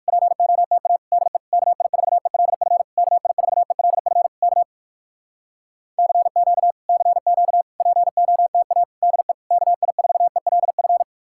Morse code, heard as a keyed tone, about 68 wpm, spaced as for 58 wpm, 700 Hz. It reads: POTA DE KI4ELF KI4ELF K CQ CQ POTA DE KI4ELF